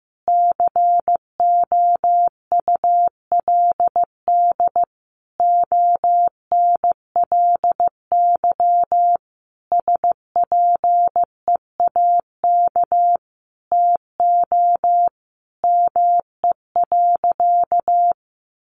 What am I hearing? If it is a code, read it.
Morse code, heard as a keyed tone, about 15 wpm, 700 Hz. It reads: COULD ONLY SPEAK TO ME.
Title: Sentences from Top 1100 Words 15wpm